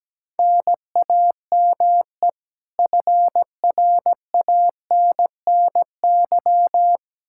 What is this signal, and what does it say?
Morse code, heard as a keyed tone, about 17 wpm, 700 Hz. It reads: NAME FRANNY
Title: Practice QSO - Exchange 1 - Beginning - 17wpm